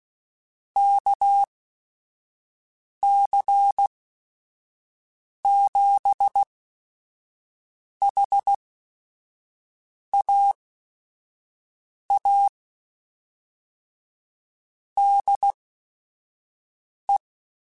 Morse code, heard as a keyed tone, about 16 wpm, spaced as for 5 wpm, 800 Hz. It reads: KC7HAA DE